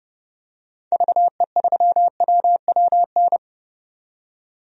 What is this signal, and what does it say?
Morse code, heard as a keyed tone, about 30 wpm, 700 Hz. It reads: VE3WWN
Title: DXCC-100 Call Signs - 6 Characters - 30wpm